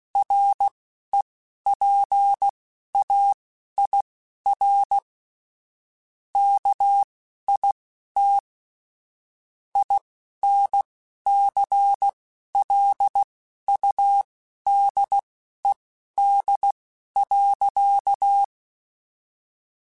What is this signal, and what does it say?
Morse code, heard as a keyed tone, about 16 wpm, spaced as for 11 wpm, 800 Hz. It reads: REPAIR KIT INCLUDED.